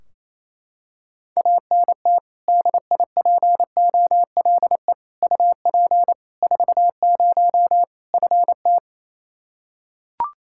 Morse code, heard as a keyed tone, about 28 wpm, 700 Hz. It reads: ANT DIPOLE UP 40 FT
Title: K1USN SST QSO Practice 28wpm